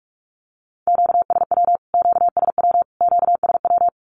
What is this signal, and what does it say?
Morse code, heard as a keyed tone, about 44 wpm, 700 Hz. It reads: QSW QSW QSW
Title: Q-Codes Repeated 3x 45wpm